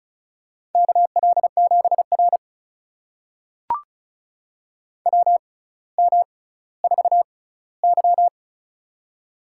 Morse code, heard as a keyed tone, about 36 wpm, spaced as for 18 wpm, 700 Hz. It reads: KL7R WM4Y
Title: USA Call Signs - 4 Characters - 6x Character Spacing - 35wpm